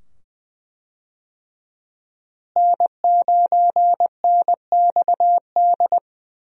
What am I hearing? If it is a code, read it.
Morse code, heard as a keyed tone, about 20 wpm, 700 Hz. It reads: N9NXD
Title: USA Call Signs - 5 Characters - Repeated 2x - 20wpm